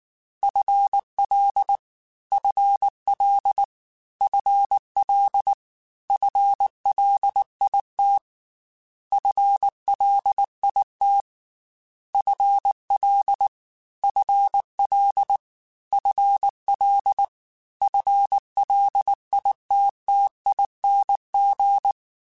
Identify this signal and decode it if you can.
Morse code, heard as a keyed tone, about 19 wpm, 800 Hz. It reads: FL FL FL FLIT FLIT FL FL FL FLITTING
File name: FL / FL mp3s / FL Prefixes 3x2 @18